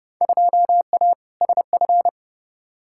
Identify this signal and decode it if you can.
Morse code, heard as a keyed tone, about 30 wpm, 700 Hz. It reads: 2A SF
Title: Field Day Exchanges - 30wpm